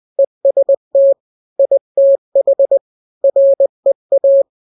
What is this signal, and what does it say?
Morse code, heard as a keyed tone, about 20 wpm, 550 Hz. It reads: EST ITH REA
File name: n-grams / 3-grams 20 18 WPM